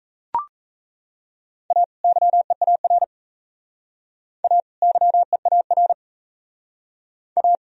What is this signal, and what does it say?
Morse code, heard as a keyed tone, about 34 wpm, 700 Hz. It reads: A YEAR A YEAR A
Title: Speed-Racing - Sets of 2 Words 28wpm